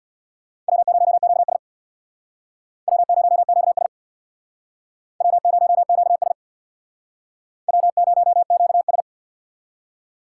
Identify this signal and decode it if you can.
Morse code, heard as a keyed tone, about 61 wpm, 700 Hz. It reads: W0QS W0QS W0QS W0QS
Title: Speed Racing - USA 1x2 Call Signs 50wpm